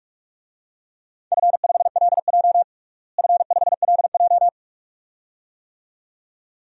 Morse code, heard as a keyed tone, about 42 wpm, 700 Hz. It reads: F5LJ F5LJ
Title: DXCC-100 Call Signs - 4 Characters - Repeated 2x - 45wpm